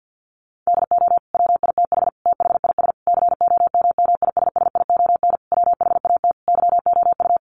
Morse code, heard as a keyed tone, about 50 wpm, 700 Hz. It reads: DO WITH THIS COMMISSION WHAT YOU
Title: Sentences from Top 1400 Words 50wpm